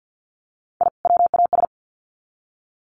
Morse code, heard as a keyed tone, about 50 wpm, 700 Hz. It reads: I WAS
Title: Sets of 2 Words 50wpm